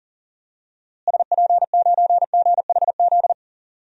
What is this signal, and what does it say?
Morse code, heard as a keyed tone, about 40 wpm, 700 Hz. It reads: SP9GHZ
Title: DXCC-100 Call Signs - 6 Characters - Repeated 2x - 40wpm